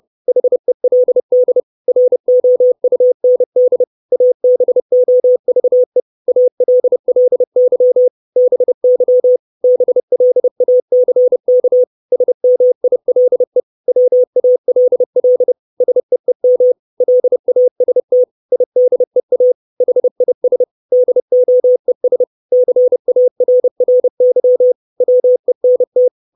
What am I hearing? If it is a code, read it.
Morse code, heard as a keyed tone, about 30 wpm, 500 Hz. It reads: HELD ROUND ABOVE ALLY BY BLACK SMILE WALL SEEM LAST IDEA HIS DOES CARRY WENT
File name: 632 Words 30WPM